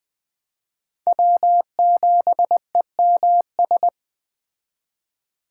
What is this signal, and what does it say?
Morse code, heard as a keyed tone, about 21 wpm, 700 Hz. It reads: W7EMS
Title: Speed Racing - USA 1x3 Call Signs 17wpm